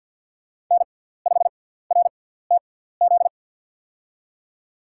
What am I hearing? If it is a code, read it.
Morse code, heard as a keyed tone, about 52 wpm, spaced as for 17 wpm, 700 Hz. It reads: N5RTZ
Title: USA Call Signs - 5 Characters - 6x Character Spacing - 50wpm